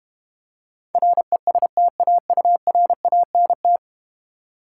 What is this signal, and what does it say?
Morse code, heard as a keyed tone, about 31 wpm, 700 Hz. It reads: RESTAURANT
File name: Easy Words - Speed-Racing with Hints - 25wpm